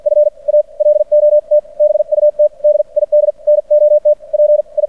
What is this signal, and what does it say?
Morse code, heard as a keyed tone, about 56 wpm, spaced as for 43 wpm, 600 Hz. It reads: F A LOT BUT DID NOT PI
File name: archived / CRSnet